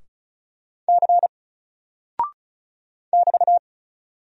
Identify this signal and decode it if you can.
Morse code, heard as a keyed tone, about 36 wpm, 700 Hz. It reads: C <BT>
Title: Single Character (Letter:Number:Punctuation:Prosign) Alt 35wpm